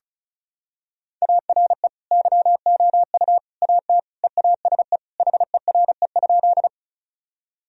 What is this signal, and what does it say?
Morse code, heard as a keyed tone, about 34 wpm, 700 Hz. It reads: ARE YOU AT EASE HERE?
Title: Sentences from Top 1000 Words - Review Part 1 - 35wpm